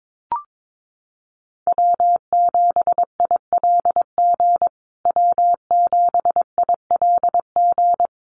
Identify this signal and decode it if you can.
Morse code, heard as a keyed tone, about 22 wpm, 700 Hz. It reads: W7ILG W7ILG
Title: USA Call Signs Repeated 3x 22wpm